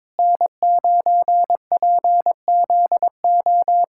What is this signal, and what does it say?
Morse code, heard as a keyed tone, about 22 wpm, 700 Hz. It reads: N9PZO
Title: USA Callsign DE Callsign - 5 Characters - 22wpm